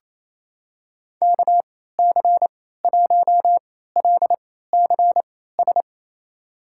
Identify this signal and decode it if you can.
Morse code, heard as a keyed tone, about 28 wpm, spaced as for 16 wpm, 700 Hz. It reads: KC1LCS
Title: USA Call Signs - 6 Characters - 3x Character Spacing - 28wpm